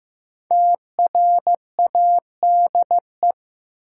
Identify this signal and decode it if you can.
Morse code, heard as a keyed tone, about 15 wpm, 700 Hz. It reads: TRADE
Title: Sets of 5 Words Encore - New Word Drill - 15wpm